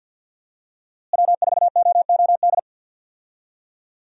Morse code, heard as a keyed tone, about 52 wpm, 700 Hz. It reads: W4OOD
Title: USA Call Signs - 5 Characters - Repeated 2x - 50wpm